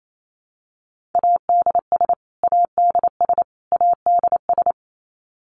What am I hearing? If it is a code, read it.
Morse code, heard as a keyed tone, about 28 wpm, 700 Hz. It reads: ADS ADS ADS
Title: Abbreviations Repeated 3x - 28wpm